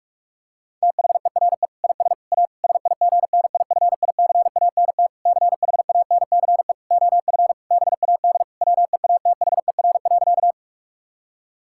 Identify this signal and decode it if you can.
Morse code, heard as a keyed tone, about 46 wpm, 700 Hz. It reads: THERE IS A SIGNIFICANT CHANCE OF BAD WEATHER.
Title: Sentences from Top 1000 Words - Review Part 6 - 45wpm